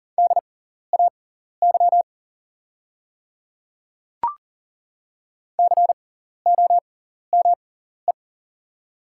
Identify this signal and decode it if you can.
Morse code, heard as a keyed tone, about 40 wpm, spaced as for 14 wpm, 700 Hz. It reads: DAY COME
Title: 6x Character Spacing - Top 100 Words - 40wpm